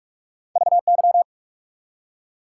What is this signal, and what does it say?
Morse code, heard as a keyed tone, about 47 wpm, 700 Hz. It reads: VY
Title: Top 180 QSO Elements 45wpm